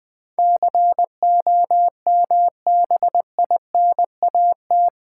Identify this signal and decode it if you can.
Morse code, heard as a keyed tone, about 20 wpm, 700 Hz. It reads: COMBINAT
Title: Build A Word - 4 Letters At A Time - 11-Letter Words - 20wpm